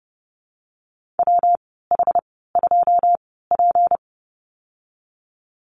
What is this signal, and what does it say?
Morse code, heard as a keyed tone, about 30 wpm, spaced as for 17 wpm, 700 Hz. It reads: WH2P
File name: Lesson 37g - Letter x - Call Signs 30wpm